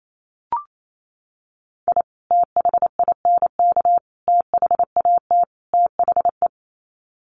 Morse code, heard as a keyed tone, about 28 wpm, 700 Hz. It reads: I THINK THAT THE
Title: Sets of 4 Words 28wpm